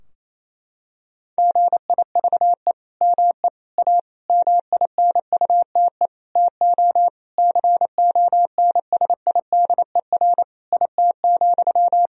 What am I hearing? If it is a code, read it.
Morse code, heard as a keyed tone, about 28 wpm, 700 Hz. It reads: GIVE ME A MINUTE TO CONSIDER IT,
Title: Sentences from Top 1000 Words - Review Part 4 - 28wpm